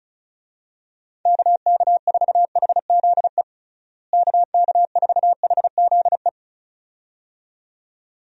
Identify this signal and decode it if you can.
Morse code, heard as a keyed tone, about 36 wpm, 700 Hz. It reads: KK4HZE KK4HZE
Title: USA Call Signs - 6 Characters - Repeated 2x - 35wpm